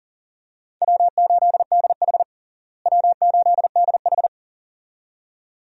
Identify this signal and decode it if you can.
Morse code, heard as a keyed tone, about 40 wpm, 700 Hz. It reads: W8DH W8DH
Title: USA Call Signs - 4 Characters - Repeated 2x - 40wpm